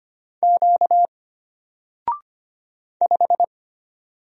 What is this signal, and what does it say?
Morse code, heard as a keyed tone, about 25 wpm, 700 Hz. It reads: Q 5
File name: Single-Letter-Number-25wpm